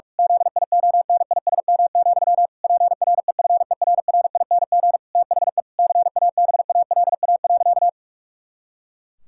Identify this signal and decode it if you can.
Morse code, heard as a keyed tone, about 46 wpm, 700 Hz. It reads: ZIONISM, PREFERRING THE CABALA.
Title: Pangram Challenge v2 - 45wpm